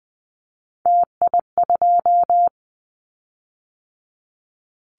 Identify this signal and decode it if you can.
Morse code, heard as a keyed tone, about 20 wpm, 700 Hz. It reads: TI2
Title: Callsign Prefixes for Top 100 DXCC - 20wpm